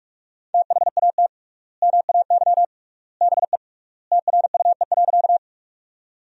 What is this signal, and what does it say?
Morse code, heard as a keyed tone, about 41 wpm, 700 Hz. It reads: THAT MAY BE TRUE.
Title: 3x Word-Spacing - Sentences from Top 400 Words - 45wpm